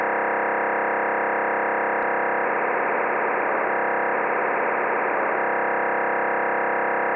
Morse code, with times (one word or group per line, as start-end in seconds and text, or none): none